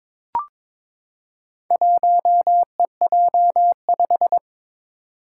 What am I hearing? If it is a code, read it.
Morse code, heard as a keyed tone, about 22 wpm, 700 Hz. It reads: 1EJ5
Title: Four Challenging Letter-Number Combinations 22wpm